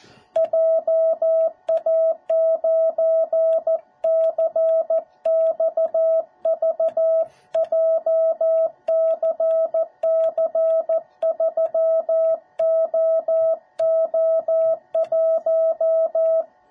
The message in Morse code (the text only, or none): JA9CXV JCC3OO1